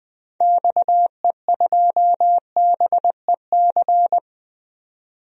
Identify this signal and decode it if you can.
Morse code, heard as a keyed tone, about 21 wpm, 700 Hz. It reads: XE2BEC
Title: DXCC-100 Call Signs - 6 Characters - 20wpm